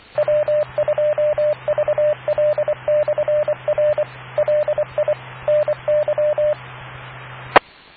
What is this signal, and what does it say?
Morse code, heard as a keyed tone, about 24 wpm, 600 Hz, with radio noise. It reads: W2VL/R LI NY E